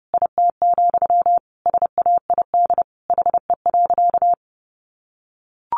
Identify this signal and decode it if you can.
Morse code, heard as a keyed tone, about 30 wpm, 700 Hz. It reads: IT, SAID HE.
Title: Sentences from Top 1100 Words 30wpm